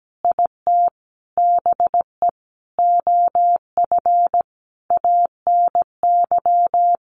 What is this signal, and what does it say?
Morse code, heard as a keyed tone, about 17 wpm, 700 Hz. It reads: IT BE OF ANY
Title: Sentences from Top 1200 Words 17wpm